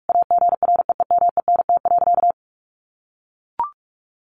Morse code, heard as a keyed tone, about 44 wpm, 700 Hz. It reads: AGREEMENT.
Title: Sentences from Top 1500 Words 45wpm